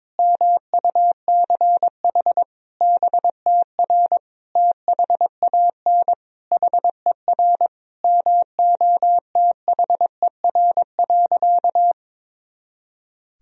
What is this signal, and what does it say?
Morse code, heard as a keyed tone, about 22 wpm, 700 Hz. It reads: MUCH BTR THAN HER MOTHER.
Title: Sentences from Top 400 Words and Abbreviations 22wpm